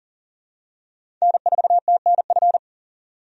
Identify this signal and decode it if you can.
Morse code, heard as a keyed tone, about 40 wpm, 700 Hz. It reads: N4TNF
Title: USA Call Signs - 5 Characters - 40wpm